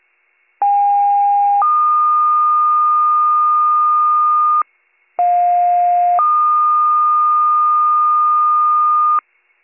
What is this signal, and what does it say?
Two-tone page, heard as a one-second tone, then a three-second tone, about 810 then 1220 Hz, about 710 then 1220 Hz. It is heard from 0.6 to 9.2 s.